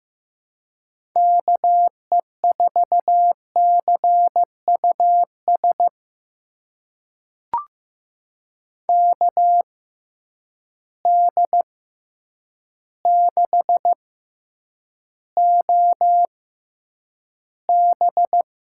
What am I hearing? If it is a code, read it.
Morse code, heard as a keyed tone, about 15 wpm, spaced as for 8 wpm, 700 Hz. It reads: KE4CUS KD6OB
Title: USA Call Signs - 6 Characters - 6x Character Spacing - 15wpm